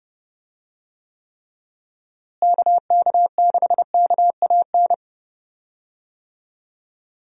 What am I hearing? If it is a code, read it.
Morse code, heard as a keyed tone, about 31 wpm, 700 Hz. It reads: KK6KAN
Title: USA Call Signs - 6 Characters - Repeated 2x - 30wpm